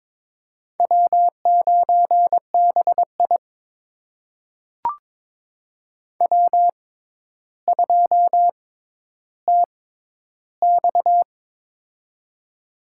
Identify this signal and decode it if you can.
Morse code, heard as a keyed tone, about 21 wpm, spaced as for 11 wpm, 700 Hz. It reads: W9BI W2TX
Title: USA Call Signs - 4 Characters - 6x Character Spacing - 22wpm